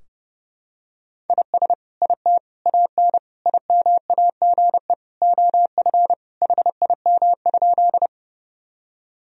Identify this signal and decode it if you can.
Morse code, heard as a keyed tone, about 32 wpm, 700 Hz. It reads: IS IT AN IMAGE OF HIM?
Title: Sentences from Top 1000 Words - Review Part 4 - 30wpm